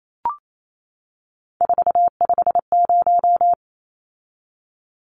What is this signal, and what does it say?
Morse code, heard as a keyed tone, about 28 wpm, 700 Hz. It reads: 450
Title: Numbers - 3-Digit 28wpm